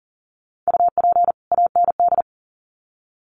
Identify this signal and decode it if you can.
Morse code, heard as a keyed tone, about 40 wpm, 700 Hz. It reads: UP AND
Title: Sets of 2 Words 40wpm